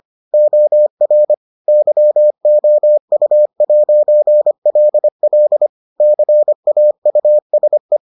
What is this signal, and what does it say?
Morse code, heard as a keyed tone, about 25 wpm, 600 Hz. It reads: OR YOU'LL CAUSE